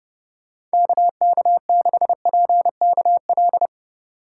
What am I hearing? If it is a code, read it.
Morse code, heard as a keyed tone, about 30 wpm, 700 Hz. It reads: KK6PKL
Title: USA Call Signs - 3x Character Spacing - 30wpm